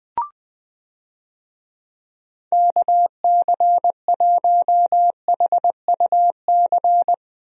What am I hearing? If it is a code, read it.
Morse code, heard as a keyed tone, about 20 wpm, 700 Hz. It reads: KC1HUC
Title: USA Call Signs - 6 Characters - Repeated 2x - 20wpm